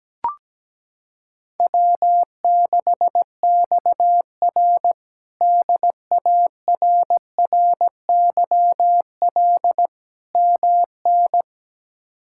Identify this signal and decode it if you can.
Morse code, heard as a keyed tone, about 17 wpm, 700 Hz. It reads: W6XR DARRYL MN
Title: K1USN SST QSO Practice - 17wpm